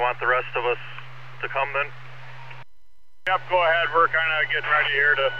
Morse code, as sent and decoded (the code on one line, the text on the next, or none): none